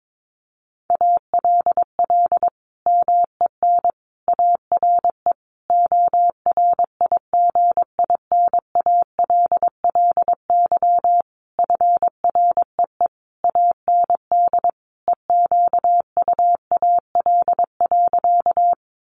Morse code, heard as a keyed tone, about 22 wpm, 700 Hz. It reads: ALL MEN ARE ORIGINALLY FREE AND EQUAL.
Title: Sentences from Top 1500 Words 22wpm